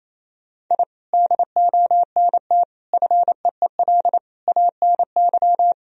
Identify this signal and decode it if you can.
Morse code, heard as a keyed tone, about 28 wpm, 700 Hz. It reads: I DONT FEEL ANY
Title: Sentences from Top 1000 Words - Review Part 3 - 28wpm